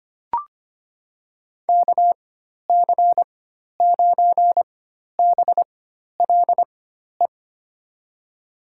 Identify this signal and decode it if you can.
Morse code, heard as a keyed tone, about 25 wpm, spaced as for 12 wpm, 700 Hz. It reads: KC9BLE
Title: USA Call Signs - 4x Character Spacing - 25wpm